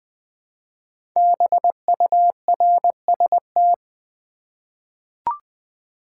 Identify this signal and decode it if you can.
Morse code, heard as a keyed tone, about 20 wpm, 700 Hz. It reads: BURST
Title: 5 Letter Words - First 3 Letters Given - 20wpm